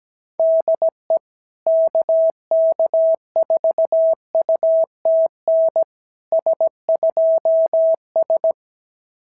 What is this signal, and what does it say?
Morse code, heard as a keyed tone, about 17 wpm, 690 Hz. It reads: DE KK4UTN S2S
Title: SOTA QSO Practice 17wpm